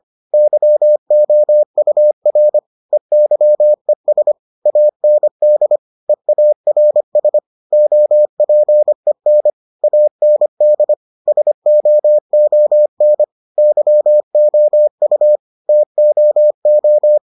Morse code, heard as a keyed tone, about 25 wpm, 600 Hz. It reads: YOUR EYES AND EARS OPEN AND SOON YOU TOO